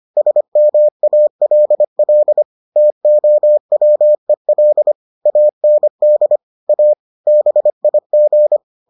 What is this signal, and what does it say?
Morse code, heard as a keyed tone, about 25 wpm, 600 Hz. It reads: SMALL TOWEL AND A BIG